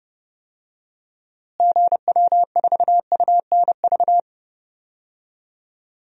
Morse code, heard as a keyed tone, about 32 wpm, 700 Hz. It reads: GW4UNV
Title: DXCC-100 Call Signs - 6 Characters - Repeated 3x - 30wpm